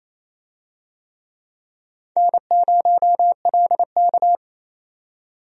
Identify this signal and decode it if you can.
Morse code, heard as a keyed tone, about 28 wpm, 700 Hz. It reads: N0LK
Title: USA Call Signs - 4 Characters - 28wpm